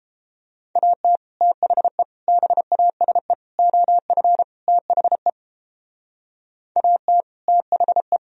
Morse code, heard as a keyed tone, about 29 wpm, 700 Hz. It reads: AT THE BASE OF THE AT THE